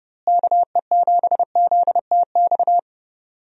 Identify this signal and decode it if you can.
Morse code, heard as a keyed tone, about 31 wpm, 700 Hz. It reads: KE7ZTX
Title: USA Call Signs - 6 Characters - Repeated 3x - 30wpm